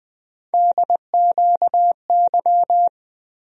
Morse code, heard as a keyed tone, about 20 wpm, 700 Hz. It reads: DQY
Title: USA Call Sign Postfixes - 3 Letters 20wpm